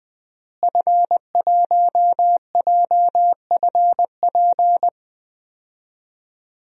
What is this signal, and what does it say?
Morse code, heard as a keyed tone, about 21 wpm, 700 Hz. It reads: F1JFP
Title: DXCC-100 Call Signs - 5 Characters - Repeated 2x - 20wpm